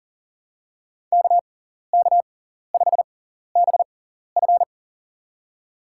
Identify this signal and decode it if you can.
Morse code, heard as a keyed tone, about 38 wpm, spaced as for 14 wpm, 700 Hz. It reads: KK5BF
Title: USA Call Signs - 5 Characters - 6x Character Spacing - 40wpm